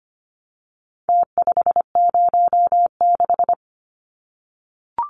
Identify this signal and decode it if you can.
Morse code, heard as a keyed tone, about 25 wpm, 700 Hz. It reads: T506
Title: Numbers - Contest Style - 4-Digit 25wpm